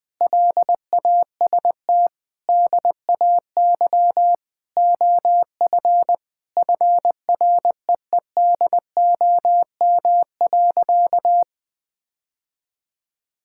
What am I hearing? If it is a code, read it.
Morse code, heard as a keyed tone, about 19 wpm, 700 Hz. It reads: LAST DAY OF FREEDOM.
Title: Sentences from Top 1000 Words - Review Part 5 - 20wpm